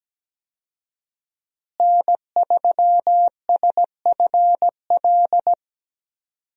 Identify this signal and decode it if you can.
Morse code, heard as a keyed tone, about 17 wpm, 700 Hz. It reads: N3SFL
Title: USA Call Signs - 5 Characters - Repeated 3x - 17wpm